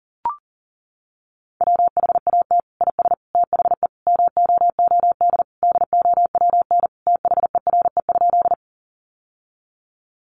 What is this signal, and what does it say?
Morse code, heard as a keyed tone, about 40 wpm, 700 Hz. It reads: WHAT IS THE MOOD DOWN THERE?
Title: Sentences from Top 1100 Words 40wpm